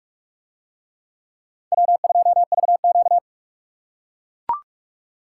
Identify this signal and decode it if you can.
Morse code, heard as a keyed tone, about 43 wpm, 700 Hz. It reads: W2VQ